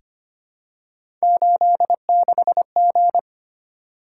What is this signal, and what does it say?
Morse code, heard as a keyed tone, about 25 wpm, 700 Hz. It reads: 86G
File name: Three Letter-Number Combinations - 25wpm